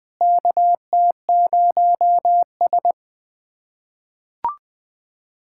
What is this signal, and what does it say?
Morse code, heard as a keyed tone, about 20 wpm, 700 Hz. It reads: KT0S
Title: USA Callsign DE Callsign - 4 Characters - 20wpm